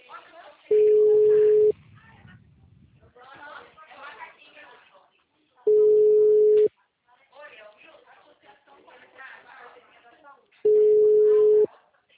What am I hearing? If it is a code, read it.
Ringback tone (Europe, 425 Hz), with speech.